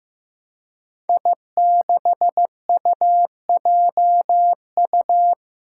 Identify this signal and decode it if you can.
Morse code, heard as a keyed tone, about 15 wpm, 700 Hz. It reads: I6UJU
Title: Five Challenging Letter-Number Combinations 15wpm